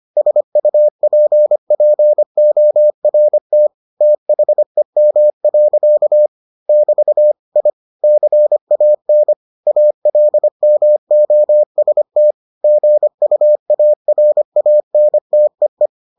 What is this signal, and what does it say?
Morse code, heard as a keyed tone, about 25 wpm, 600 Hz. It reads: SUPPORT THEM. <BT> I CAN ALMOST GUARANTEE